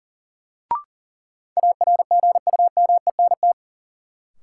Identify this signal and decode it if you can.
Morse code, heard as a keyed tone, about 40 wpm, 700 Hz. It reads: ARGUMENT
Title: Sentences from Top 1400 Words - New Word Drill - 40wpm